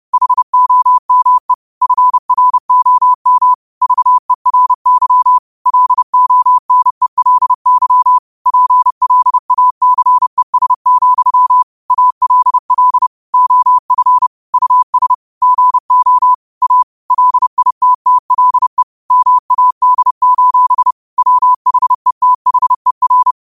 Morse code, heard as a keyed tone, about 30 wpm, 1000 Hz. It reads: HOME FROM VERY LONELY PLACES, ALL OF US GO A LITTLE MAD: WHETHER
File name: qotdfiles / 30wpm